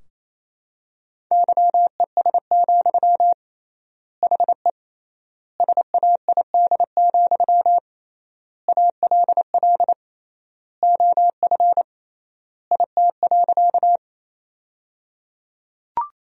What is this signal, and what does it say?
Morse code, heard as a keyed tone, about 28 wpm, 700 Hz. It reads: YES, HE SAID, ALL OF IT.
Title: 3x Word-Spacing - Sentences from Top 500 Words - 28wpm